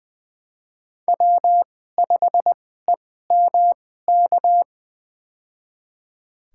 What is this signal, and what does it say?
Morse code, heard as a keyed tone, about 20 wpm, spaced as for 14 wpm, 700 Hz. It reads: W5EMK
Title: USA Call Signs - 5 Characters - 2x Character Spacing - 20wpm